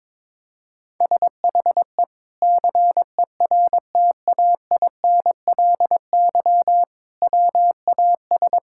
Morse code, heard as a keyed tone, about 22 wpm, 700 Hz. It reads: SHE CERTAINLY WAS
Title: Sentences from Top 1000 Words - Review Part 6 - 22wpm